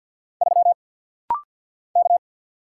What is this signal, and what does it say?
Morse code, heard as a keyed tone, about 48 wpm, 700 Hz. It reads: 3 K
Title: Single Letter-Number - Rapid-Fire 50wpm